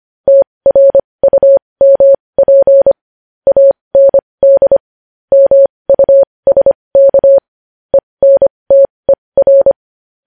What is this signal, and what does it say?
Morse code, heard as a keyed tone, about 25 wpm, spaced as for 20 wpm, 550 Hz. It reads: TRUMP AND MUSK ENTER
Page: news-headlines-in-morse-code-at-20-wpm